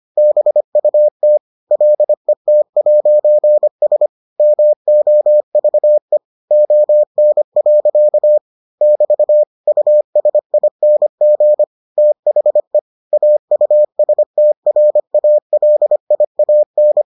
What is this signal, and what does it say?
Morse code, heard as a keyed tone, about 25 wpm, 600 Hz. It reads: BUT LET'S MOVE ON. <BT> USING THE AUSTRALIAN